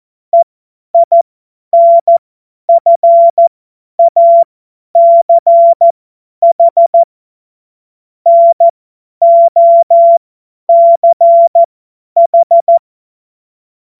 Morse code, heard as a keyed tone, about 14 wpm, spaced as for 10 wpm, 680 Hz. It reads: EINFACH NOCH
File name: Morsekurs / Witze / Witz 27. Juli 2021, 50 F70